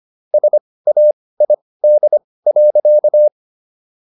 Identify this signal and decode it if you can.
Morse code, heard as a keyed tone, about 25 wpm, spaced as for 18 wpm, 600 Hz. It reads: SAID.